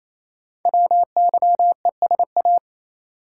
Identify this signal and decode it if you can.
Morse code, heard as a keyed tone, about 28 wpm, 700 Hz. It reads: WYESA
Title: Code Groups - 5 Letters - 28wpm